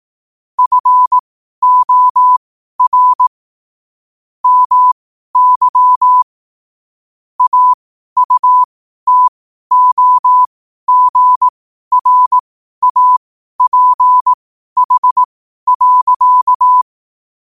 Morse code, heard as a keyed tone, about 18 wpm, spaced as for 12 wpm, 1000 Hz. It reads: FOR MY AUTOGRAPH.